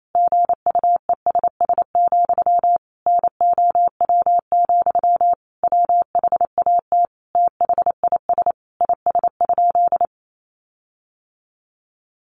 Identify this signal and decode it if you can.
Morse code, heard as a keyed tone, about 28 wpm, 700 Hz. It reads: GUESS, NOW, WHAT THIS IS?
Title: Sentences from Top 1500 Words 28wpm